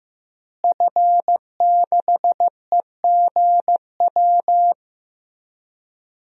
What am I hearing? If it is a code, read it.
Morse code, heard as a keyed tone, about 15 wpm, 700 Hz. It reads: F6EGW
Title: DXCC-100 Call Signs - 5 Characters - Repeated 3x - 15wpm